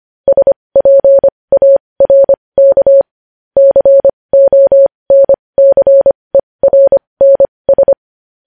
Morse code, heard as a keyed tone, about 25 wpm, spaced as for 20 wpm, 550 Hz. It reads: SPARK CONCERNS
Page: news-headlines-in-morse-code-at-20-wpm